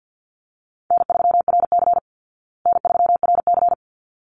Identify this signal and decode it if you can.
Morse code, heard as a keyed tone, about 50 wpm, 700 Hz. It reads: N3RC N3RC
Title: Call Signs Repeated 3x - 50wpm